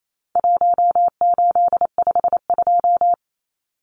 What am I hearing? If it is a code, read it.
Morse code, heard as a keyed tone, about 28 wpm, 700 Hz. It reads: 1852
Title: Numbers - 4-Digit 28wpm